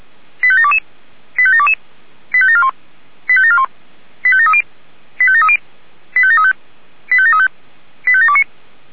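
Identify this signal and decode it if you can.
Five-tone selective call (ZVEI), heard as quick runs of five tones, tones of about 70 ms. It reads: 85722 85722 85741 85741 85730 85730 85735 85735 85729
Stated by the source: ZVEI five-tone call 85825 received on SDR.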